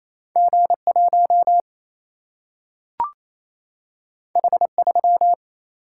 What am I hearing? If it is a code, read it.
Morse code, heard as a keyed tone, about 28 wpm, 700 Hz. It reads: G 1 H 3
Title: Single-Letter Single-Number 28wpm